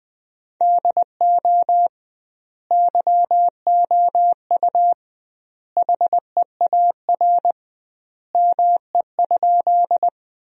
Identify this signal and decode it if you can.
Morse code, heard as a keyed tone, about 20 wpm, 700 Hz. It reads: DO YOU HEAR ME?
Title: 2x Word-Spacing - Sentences from Top 300 Words - 20wpm